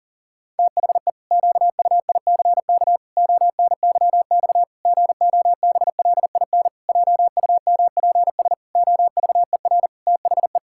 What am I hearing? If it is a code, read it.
Morse code, heard as a keyed tone, about 40 wpm, 700 Hz. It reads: THE QUICK ONYX GOBLIN JUMPS OVER THE
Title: Pangram Challenge v2 - 40wpm